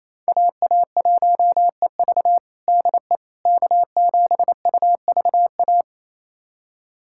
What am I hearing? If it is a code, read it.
Morse code, heard as a keyed tone, about 28 wpm, 700 Hz. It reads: AA1EV DE K7UVA
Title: USA Callsign DE Callsign - 5 Characters - 28wpm